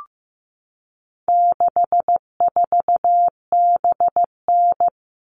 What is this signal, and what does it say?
Morse code, heard as a keyed tone, about 15 wpm, 700 Hz. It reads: 64BN
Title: All Cut Numbers - 4-Digit 15wpm